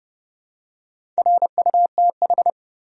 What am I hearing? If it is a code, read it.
Morse code, heard as a keyed tone, about 30 wpm, 700 Hz. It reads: RUTH
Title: USA Common Names - Max 5 Letters - 30wpm